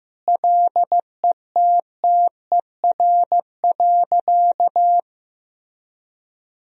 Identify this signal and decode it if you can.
Morse code, heard as a keyed tone, about 15 wpm, 700 Hz. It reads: LETTER.
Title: Sentences from Top 1000 Words - Review Part 3 - 15wpm